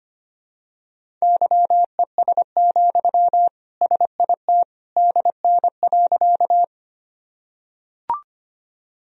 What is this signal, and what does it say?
Morse code, heard as a keyed tone, about 25 wpm, 700 Hz. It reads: YES, SIT DN.
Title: Sentences from Top 500 Words and Abbreviations 25wpm